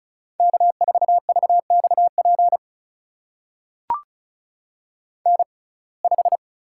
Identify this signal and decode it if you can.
Morse code, heard as a keyed tone, about 36 wpm, spaced as for 25 wpm, 700 Hz. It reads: K4VXP N5
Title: USA Call Signs - 5 Characters - 6x Character Spacing - 35wpm